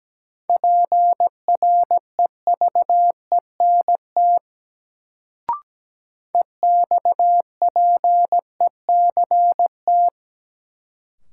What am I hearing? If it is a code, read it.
Morse code, heard as a keyed tone, about 17 wpm, 700 Hz. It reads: PREVENT EXPECT
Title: Sentences from Top 1100 Words - New Word Drill Alt - 17wpm